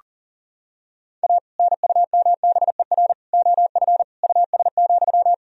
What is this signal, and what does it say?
Morse code, heard as a keyed tone, about 40 wpm, 700 Hz. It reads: A NUMBER OF US,
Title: Sentences from Top 1000 Words - Review Part 1 - 40wpm